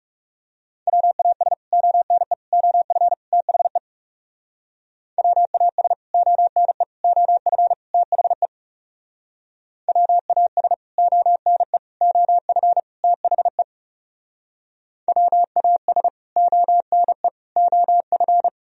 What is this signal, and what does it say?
Morse code, heard as a keyed tone, about 38 wpm, 700 Hz. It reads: WAS ONE OF THE WAS ONE OF THE WAS ONE OF THE WAS ONE OF